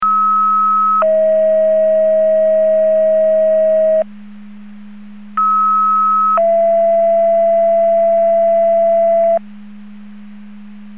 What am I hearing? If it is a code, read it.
Two-tone page, heard as a one-second tone, then a three-second tone, about 1290 then 640 Hz, about 1290 then 670 Hz. It is heard from 0.0 to 9.4 s.
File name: Portage 154.385 Paging